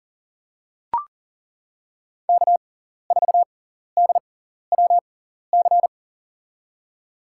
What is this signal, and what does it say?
Morse code, heard as a keyed tone, about 40 wpm, spaced as for 14 wpm, 700 Hz. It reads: K4DWC
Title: USA Call Signs - 5 Characters - 6x Character Spacing - 40wpm